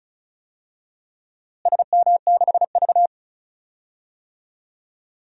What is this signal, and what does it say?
Morse code, heard as a keyed tone, about 34 wpm, 700 Hz. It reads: SM6V
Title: DXCC-100 Call Signs - 4 Characters - Repeated 3x - 35wpm